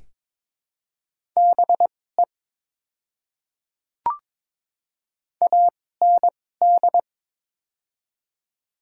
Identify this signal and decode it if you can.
Morse code, heard as a keyed tone, about 22 wpm, spaced as for 16 wpm, 700 Hz. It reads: BE AND
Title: 2x Character Spacing - Top 100 Words - 22wpm